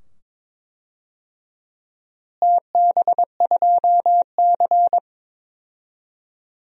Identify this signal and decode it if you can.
Morse code, heard as a keyed tone, about 21 wpm, 700 Hz. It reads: TB2C
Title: DXCC-100 Call Signs - 4 Characters - Repeated 2x - 22wpm